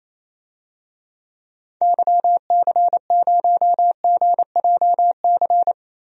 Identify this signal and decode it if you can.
Morse code, heard as a keyed tone, about 27 wpm, 700 Hz. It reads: YC0GJC
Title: DXCC-100 Call Signs - 6 Characters - Repeated 3x - 28wpm